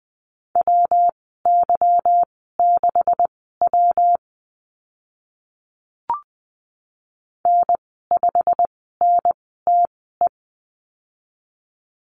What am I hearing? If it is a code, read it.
Morse code, heard as a keyed tone, about 20 wpm, spaced as for 15 wpm, 700 Hz. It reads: WY6W N5NTE
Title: 147 Lesson 24g - Letter y - Call Signs 20wpm